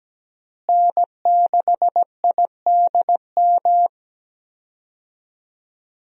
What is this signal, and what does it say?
Morse code, heard as a keyed tone, about 17 wpm, 700 Hz. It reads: N6IDM
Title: USA Call Signs - 5 Characters - Repeated 3x - 17wpm